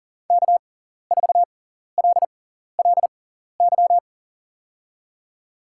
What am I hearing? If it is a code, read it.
Morse code, heard as a keyed tone, about 38 wpm, spaced as for 14 wpm, 700 Hz. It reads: K4LLY
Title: USA Call Signs - 6x Character Spacing - 40wpm